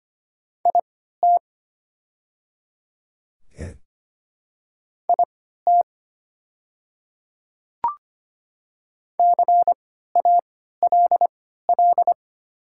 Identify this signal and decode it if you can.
Morse code, heard as a keyed tone, about 25 wpm, spaced as for 14 wpm, 690 Hz. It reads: IT IT CALL
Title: 3x Character Spacing - Top 100 Words - 25wpm